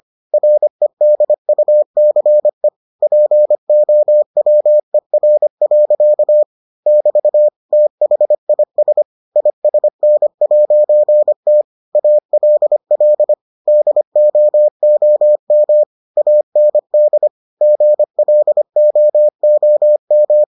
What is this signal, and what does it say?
Morse code, heard as a keyed tone, about 25 wpm, 600 Hz. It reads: REDUCE POWER. <BT> THIS ISN'T ALL DOOM AND GLOOM